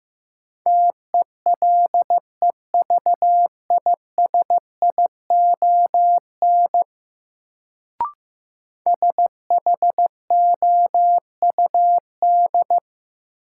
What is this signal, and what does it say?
Morse code, heard as a keyed tone, about 15 wpm, 700 Hz. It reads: TELEVISION SHOUD
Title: Sentences from Top 1000 Words - Review Part 4 - New Word Drill - 15wpm